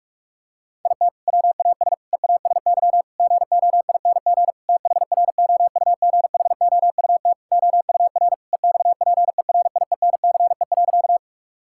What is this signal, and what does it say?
Morse code, heard as a keyed tone, about 46 wpm, 700 Hz. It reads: IT WAS EASY GOING THROUGHOUT OUR EXPERIENCE.
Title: Sentences from Top 1000 Words - Review Part 5 - 45wpm